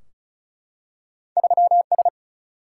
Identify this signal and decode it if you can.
Morse code, heard as a keyed tone, about 36 wpm, 700 Hz. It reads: 3 S
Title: Single-Letter Single-Number 35wpm